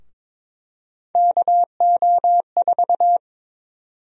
Morse code, heard as a keyed tone, about 22 wpm, 700 Hz. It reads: KO4
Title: USA Call Sign Prefixes - 22wpm